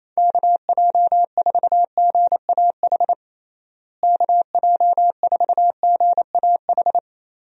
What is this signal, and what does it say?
Morse code, heard as a keyed tone, about 28 wpm, 700 Hz. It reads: KJ4GAH KJ4GAH
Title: USA Call Signs - 6 Characters - Repeated 3x - 28wpm